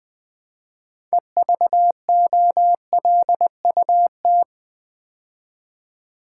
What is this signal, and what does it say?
Morse code, heard as a keyed tone, about 20 wpm, 700 Hz. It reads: EVOLUT
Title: Build A Word - 3 Letters At A Time - 9-Letter Words - 20wpm